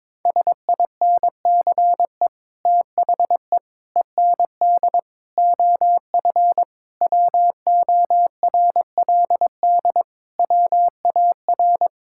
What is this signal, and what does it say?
Morse code, heard as a keyed tone, about 22 wpm, 700 Hz. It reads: SINCE THE END OF WORLD WAR
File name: practice / 22wpm / Sets of 7 Words - 22wpm